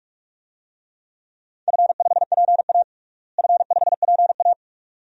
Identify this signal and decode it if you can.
Morse code, heard as a keyed tone, about 46 wpm, 700 Hz. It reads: F5PA F5PA
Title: DXCC-100 Call Signs - 4 Characters - Repeated 2x - 45wpm